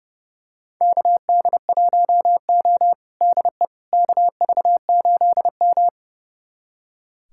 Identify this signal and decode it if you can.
Morse code, heard as a keyed tone, about 32 wpm, 700 Hz. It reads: KD1O DE KV8M
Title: USA Callsign DE Callsign - 4 Characters - 30wpm